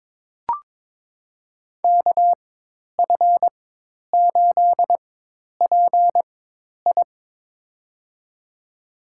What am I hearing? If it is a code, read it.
Morse code, heard as a keyed tone, about 22 wpm, spaced as for 10 wpm, 700 Hz. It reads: KF8PI
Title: USA Call Signs - 5 Characters - 4x Character Spacing - 22wpm